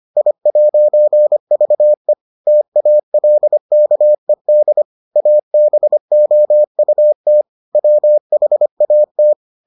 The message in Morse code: I'VE TALKED ABOUT WHAT

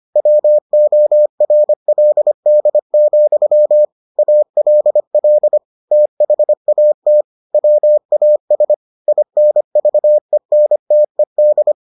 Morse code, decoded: WORLD, ALL THAT WAS INVENTED

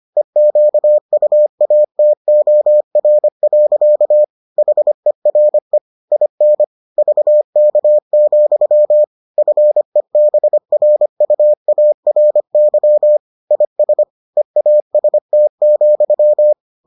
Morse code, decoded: EQUATOR. HERE IN VK, FEBRUARY IS EAST,

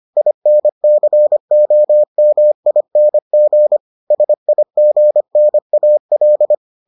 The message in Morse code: INCOMING SIGNAL

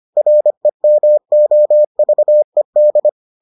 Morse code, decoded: REMOVED

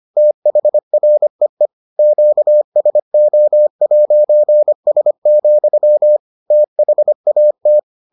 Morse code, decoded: THREE QSO'S, THAT

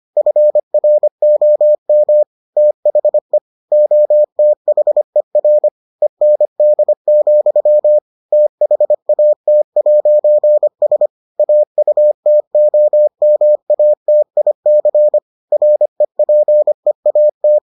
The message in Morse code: FROM THE OTHER END, THAT'S AUTOMATIC REPEAT

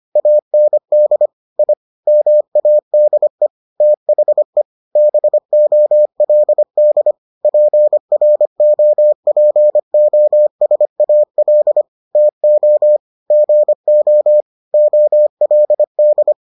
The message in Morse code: AND I MADE THE BOLD PROPOSAL TO GO OLD

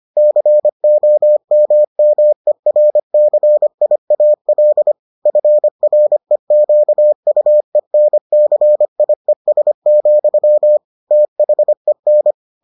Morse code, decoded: COMMERCIAL FREQUENCIES, THEN

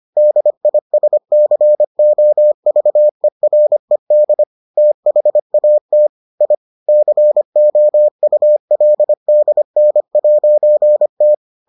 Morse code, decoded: DISCOVERED THAT I COULDN'T

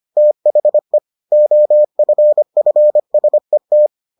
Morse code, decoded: THE OFFSET

